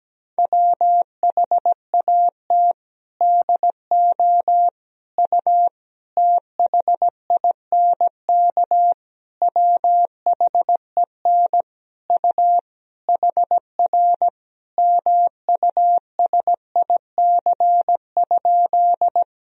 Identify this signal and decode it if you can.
Morse code, heard as a keyed tone, about 17 wpm, 700 Hz. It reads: WHAT DO U THINK WHEN U HR MUSIC?
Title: Sentences from Top 300 Words and Abbreviations 17wpm